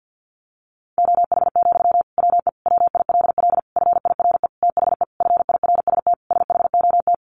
Morse code, heard as a keyed tone, about 50 wpm, 700 Hz. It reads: OH, WE WILL FIRE THE FIRST SHOT
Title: Sentences from Top 1100 Words 50wpm